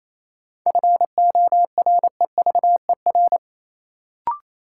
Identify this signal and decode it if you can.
Morse code, heard as a keyed tone, about 28 wpm, 700 Hz. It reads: FOREVER
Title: Sentences from Top 1600 Words - New Word Drill - 28wpm